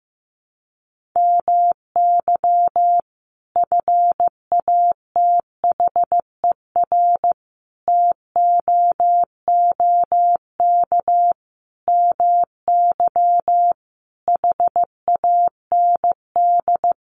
Morse code, heard as a keyed tone, about 15 wpm, 700 Hz. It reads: MY FATHER TOOK MY HAND
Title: Sentences from Top 1500 Words 15wpm